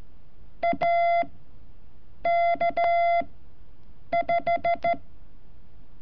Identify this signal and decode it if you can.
Morse code, heard as a keyed tone, about 14 wpm, spaced as for 7 wpm, 680 Hz. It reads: AK5
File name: curso / ejercicio 31